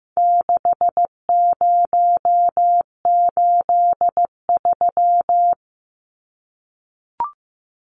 Morse code, heard as a keyed tone, about 15 wpm, 700 Hz. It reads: 6083
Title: Numbers - 4-Digit 15wpm